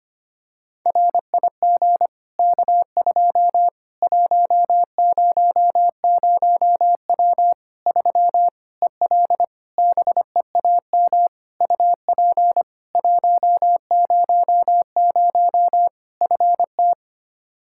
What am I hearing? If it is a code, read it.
Morse code, heard as a keyed tone, about 25 wpm, 700 Hz. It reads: RIG K2 100W 3 EL BEAM UP 100 FT
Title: K1USN SST QSO Practice 25wpm